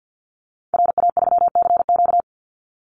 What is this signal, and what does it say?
Morse code, heard as a keyed tone, about 50 wpm, 700 Hz. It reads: RA3CQ
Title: Call Signs - 50wpm